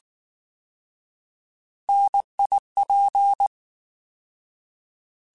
Morse code, heard as a keyed tone, about 19 wpm, 800 Hz. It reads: NIP